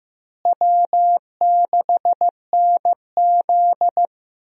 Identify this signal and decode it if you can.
Morse code, heard as a keyed tone, about 15 wpm, 700 Hz. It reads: W6NZ
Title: USA Call Signs - 4 Characters - Repeated 2x - 15wpm